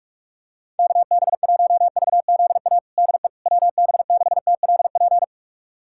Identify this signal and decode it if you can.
Morse code, heard as a keyed tone, about 46 wpm, 700 Hz. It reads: KB1VZA DE WB6TLP
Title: USA Callsign DE Callsign - 6 Characters - 45wpm